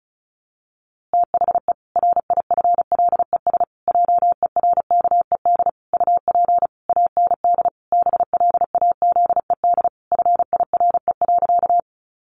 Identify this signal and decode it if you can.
Morse code, heard as a keyed tone, about 34 wpm, 700 Hz. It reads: THE RIFLES JERKED UP AND BLAZED FIRE.
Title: A Sound Of Thunder 35wpm